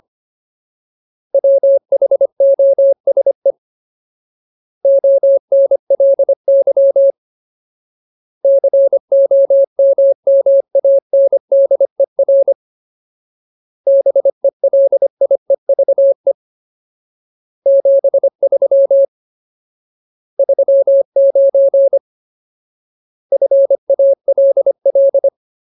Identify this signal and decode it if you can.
Morse code, heard as a keyed tone, about 25 wpm, 550 Hz. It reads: WHOSE ONLY COMMANDER BELIEVE 73 39 FALL
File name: Ultimate Word List / 25WPM Ultimate